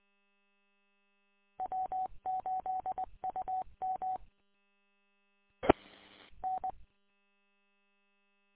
Morse code, heard as a keyed tone, about 20 wpm, 750 Hz. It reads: W8UM E N
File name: w8um- -combined